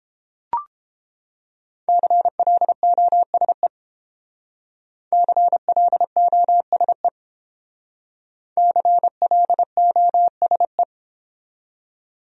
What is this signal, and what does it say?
Morse code, heard as a keyed tone, about 30 wpm, 700 Hz. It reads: CLOSE CLOSE CLOSE